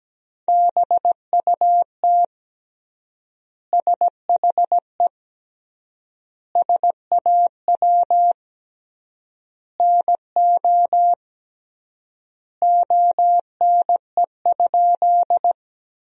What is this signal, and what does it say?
Morse code, heard as a keyed tone, about 17 wpm, 700 Hz. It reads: BUT SHE SAW NO ONE?
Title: 3x Word-Spacing - Sentences from Top 300 Words - 17wpm